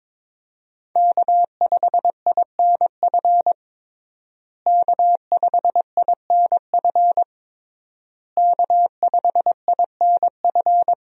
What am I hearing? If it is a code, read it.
Morse code, heard as a keyed tone, about 22 wpm, 700 Hz. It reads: K5INF K5INF K5INF
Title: USA Call Signs - 5 Characters - Repeated 3x - 22wpm